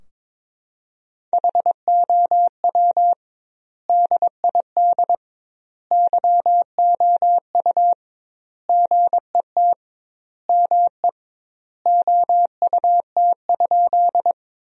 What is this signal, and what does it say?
Morse code, heard as a keyed tone, about 22 wpm, 700 Hz. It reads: HOW DID YOU GET ME OUT?
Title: 2x Word-Spacing - Sentences from Top 200 Words - 22wpm